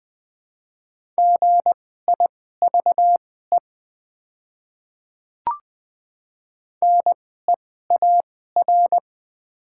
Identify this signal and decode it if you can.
Morse code, heard as a keyed tone, about 20 wpm, spaced as for 14 wpm, 700 Hz. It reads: GIVE NEAR
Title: 2x Character Spacing - Top 500 Words - 20wpm